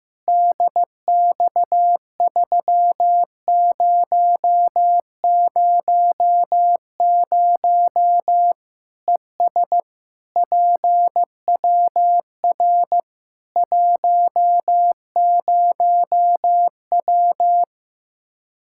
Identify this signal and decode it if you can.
Morse code, heard as a keyed tone, about 15 wpm, 700 Hz. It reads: DX3000 ES PWR 10W
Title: Practice QSO - Exchange 2 - Info - 15wpm